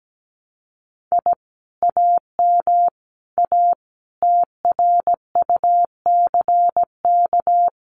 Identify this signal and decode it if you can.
Morse code, heard as a keyed tone, about 17 wpm, 700 Hz. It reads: I AM A TRUCK
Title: Common QSO Phrases Encore 17wpm